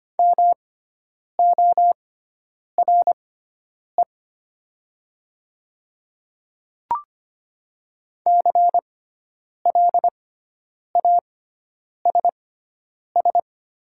Morse code, heard as a keyed tone, about 25 wpm, spaced as for 9 wpm, 700 Hz. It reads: MORE CLASS
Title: 6x Character Spacing - Top 500 Words - 25wpm